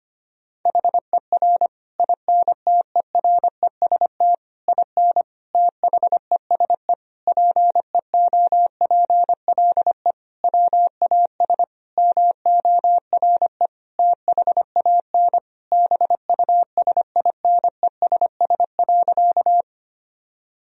Morse code, heard as a keyed tone, about 25 wpm, 700 Hz. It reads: HER INTEREST IN THESE PEOPLE WAS MORE THAN BUSINESS.
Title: Sentences from Top 1000 Words - Review Part 2 - 25wpm